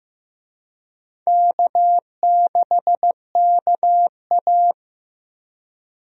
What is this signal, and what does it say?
Morse code, heard as a keyed tone, about 15 wpm, 700 Hz. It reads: K6KA
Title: USA Call Signs - 4 Characters - Repeated 2x - 15wpm